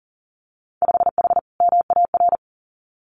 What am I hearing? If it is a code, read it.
Morse code, heard as a keyed tone, about 40 wpm, 700 Hz. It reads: 5H MAR
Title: Winter Field Day Exchanges Encore 40wpm